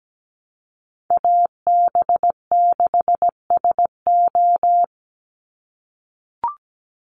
Morse code, heard as a keyed tone, about 17 wpm, 700 Hz. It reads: AB6SO
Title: NA SOTA Chaser Callsigns 17wpm